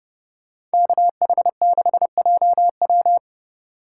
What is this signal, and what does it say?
Morse code, heard as a keyed tone, about 28 wpm, 700 Hz. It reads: KH6JW